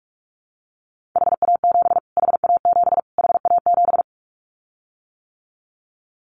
Morse code, heard as a keyed tone, about 44 wpm, 700 Hz. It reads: HA7 HA7 HA7
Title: Callsign Prefixes for Top 25 DXCC - Repeated 3x - 45wpm